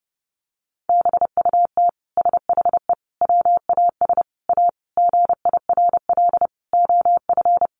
Morse code, heard as a keyed tone, about 28 wpm, 700 Hz. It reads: BUT SHE WAS A GIRL OF
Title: Sentences from Top 1400 Words 30wpm